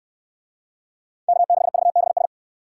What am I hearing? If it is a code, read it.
Morse code, heard as a keyed tone, about 65 wpm, 700 Hz. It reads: K6VBS